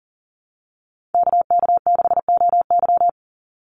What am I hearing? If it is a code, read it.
Morse code, heard as a keyed tone, about 40 wpm, 700 Hz. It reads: KK6OY
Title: USA Call Signs 40wpm